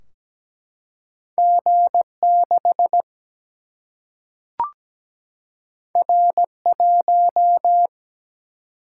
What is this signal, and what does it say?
Morse code, heard as a keyed tone, about 17 wpm, 700 Hz. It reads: G6 R1
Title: Single-Letter Single-Number 17wpm